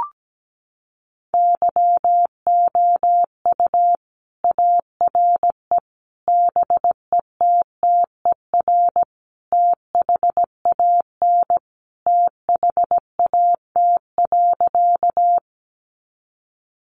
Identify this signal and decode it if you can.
Morse code, heard as a keyed tone, about 17 wpm, 700 Hz. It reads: YOU ARE BETTER THAN THAT.
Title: Sentences from Top 400 Words 17wpm